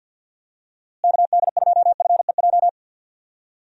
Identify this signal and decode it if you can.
Morse code, heard as a keyed tone, about 52 wpm, 700 Hz. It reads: KD2FEJ
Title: USA Call Signs - 6 Characters - Repeated 2x - 50wpm